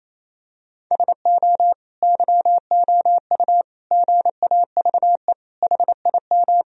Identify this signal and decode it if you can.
Morse code, heard as a keyed tone, about 27 wpm, 700 Hz. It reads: SO YOU GAVE HIM
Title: Sentences from Top 1000 Words - Review Part 2 - 28wpm